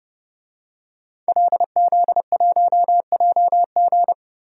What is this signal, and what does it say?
Morse code, heard as a keyed tone, about 30 wpm, 700 Hz. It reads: LZ1JG
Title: DXCC-100 Call Signs - 5 Characters - Repeated 2x - 30wpm